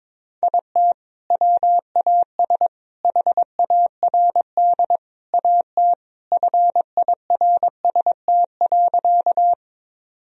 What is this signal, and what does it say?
Morse code, heard as a keyed tone, about 22 wpm, 700 Hz. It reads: IT WAS HARD AT FIRST.
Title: Sentences from Top 1000 Words - Review Part 1 - 22wpm